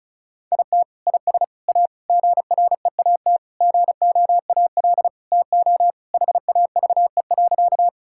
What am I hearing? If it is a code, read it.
Morse code, heard as a keyed tone, about 36 wpm, 700 Hz. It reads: IT IS A GREAT GOAL TO HAVE.
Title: Sentences from Top 1000 Words - Review Part 5 - 35wpm